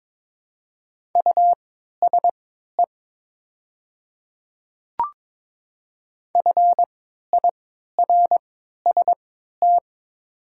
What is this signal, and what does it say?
Morse code, heard as a keyed tone, about 22 wpm, spaced as for 12 wpm, 700 Hz. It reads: USE FIRST
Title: 3x Character Spacing - Top 100 Words - 22wpm